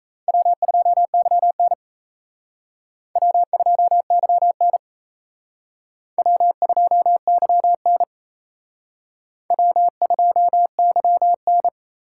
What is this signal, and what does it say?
Morse code, heard as a keyed tone, about 36 wpm, 700 Hz. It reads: W2YN W2YN W2YN W2YN